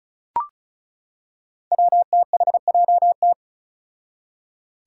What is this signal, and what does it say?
Morse code, heard as a keyed tone, about 34 wpm, 700 Hz. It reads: WTHJT
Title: Five Challenging Letter-Number Combinations 35wpm